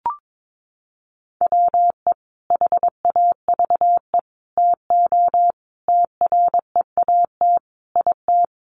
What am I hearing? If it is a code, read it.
Morse code, heard as a keyed tone, about 22 wpm, 700 Hz. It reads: WE HAVE TO TREAT IT
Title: Sentences from Top 1300 Words 22wpm